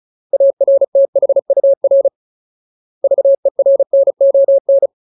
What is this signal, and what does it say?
Morse code, heard as a keyed tone, about 34 wpm, 550 Hz. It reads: ARTHUR VERNON
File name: Ultimate Word List / 35WPM Ultimate